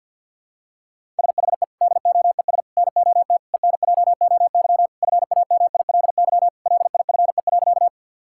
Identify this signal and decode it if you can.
Morse code, heard as a keyed tone, about 48 wpm, 690 Hz. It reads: SHE DOES NOT ENJOY FAMILY LIFE.
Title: Sentences from Top 1000 Words - Review Part 5 - 50wpm